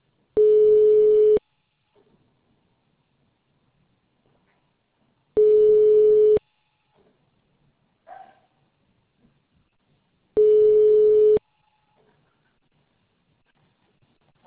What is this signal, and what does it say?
Ringback tone (Europe, 425 Hz).